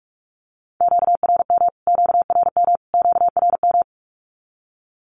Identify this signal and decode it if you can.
Morse code, heard as a keyed tone, about 44 wpm, 700 Hz. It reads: QRM QRM QRM
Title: Q-Codes Repeated 3x 45wpm